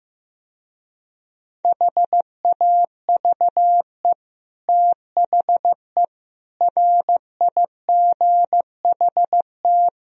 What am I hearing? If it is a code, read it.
Morse code, heard as a keyed tone, about 16 wpm, 700 Hz. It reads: HAVE THE RIGHT
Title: Speed-Racing - Sets of 4 Words 15wpm